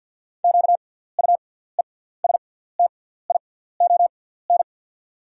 Morse code, heard as a keyed tone, about 48 wpm, spaced as for 17 wpm, 690 Hz. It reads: QUESTION
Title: 6x Character Spacing - Top 500 Words - 50wpm